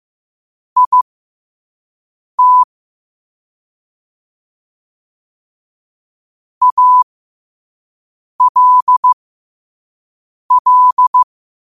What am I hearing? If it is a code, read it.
Morse code, heard as a keyed tone, about 15 wpm, spaced as for 5 wpm, 1000 Hz. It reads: IT ALL
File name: qotdfiles / 5wpm